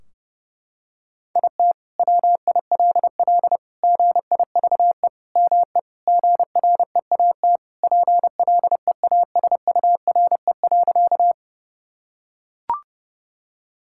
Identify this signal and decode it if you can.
Morse code, heard as a keyed tone, about 32 wpm, 700 Hz. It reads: IT WILL GIVE ME GREAT PLEASURE.
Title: Sentences from Top 1600 Words 30wpm